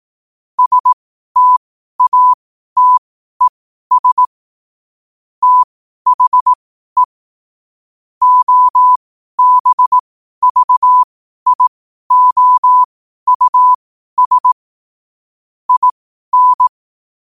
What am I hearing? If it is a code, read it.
Morse code, heard as a keyed tone, about 18 wpm, spaced as for 12 wpm, 1000 Hz. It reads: STATES THE OBVIOUS IN
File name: 12wpm